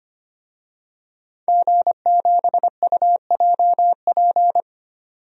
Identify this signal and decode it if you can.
Morse code, heard as a keyed tone, about 25 wpm, 700 Hz. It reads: G7UJP
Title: DXCC-100 Call Signs - 5 Characters - 25wpm